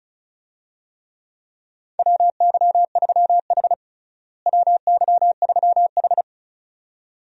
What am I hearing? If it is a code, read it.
Morse code, heard as a keyed tone, about 34 wpm, 700 Hz. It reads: WY3H WY3H
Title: USA Call Signs - 4 Characters - Repeated 2x - 35wpm